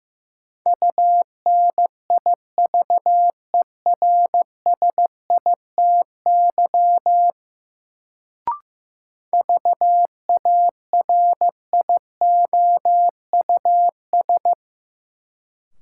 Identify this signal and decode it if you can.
Morse code, heard as a keyed tone, about 15 wpm, 700 Hz. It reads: UNIVERSITY VARIOUS
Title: Sentences from Top 1000 Words - Review Part 4 - New Word Drill - 15wpm